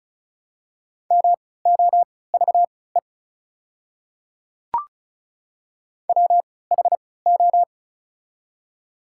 Morse code, heard as a keyed tone, about 36 wpm, spaced as for 20 wpm, 700 Hz. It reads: MOVE WHO
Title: 3x Character Spacing - Top 200 Words - 35wpm